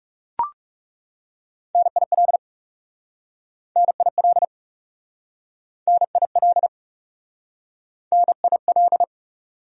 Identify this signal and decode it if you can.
Morse code, heard as a keyed tone, about 40 wpm, 700 Hz. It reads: NIL NIL NIL NIL